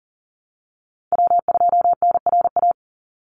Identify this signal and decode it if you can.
Morse code, heard as a keyed tone, about 40 wpm, 700 Hz. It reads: W2NRA
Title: Call Signs - 40wpm